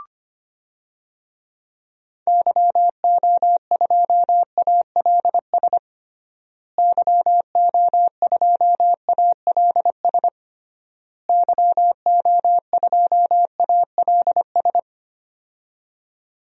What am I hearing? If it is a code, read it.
Morse code, heard as a keyed tone, about 25 wpm, 700 Hz. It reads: YO2ALS YO2ALS YO2ALS
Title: DXCC-100 Call Signs - 6 Characters - Repeated 3x - 25wpm